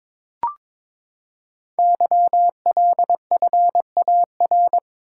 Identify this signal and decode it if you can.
Morse code, heard as a keyed tone, about 22 wpm, 700 Hz. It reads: YLFAR
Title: Code Groups - 5 Letters - 22wpm